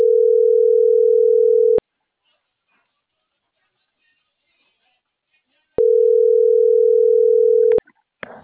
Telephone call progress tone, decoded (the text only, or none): ringback tone